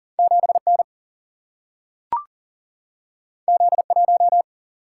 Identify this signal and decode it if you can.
Morse code, heard as a keyed tone, about 41 wpm, 700 Hz. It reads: 7N Z1
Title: Single-Letter Single-Number 40wpm